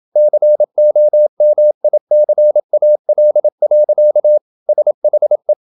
Morse code, heard as a keyed tone, about 27 wpm, 600 Hz. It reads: COMICAL. SHE
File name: Never 0027wpm